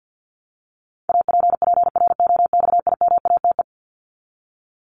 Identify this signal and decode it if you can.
Morse code, heard as a keyed tone, about 50 wpm, 700 Hz. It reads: APPROXIMATE
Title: LY Suffix - Top 100 Words with Base Words 50wpm